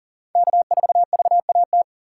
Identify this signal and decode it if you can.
Morse code, heard as a keyed tone, about 40 wpm, 700 Hz. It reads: K4VAT
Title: USA Callsign DE Callsign - 5 Characters - 40wpm